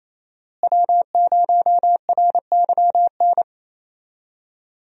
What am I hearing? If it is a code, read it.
Morse code, heard as a keyed tone, about 27 wpm, 700 Hz. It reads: W0RYN
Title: USA Call Signs - 5 Characters - Repeated 3x - 28wpm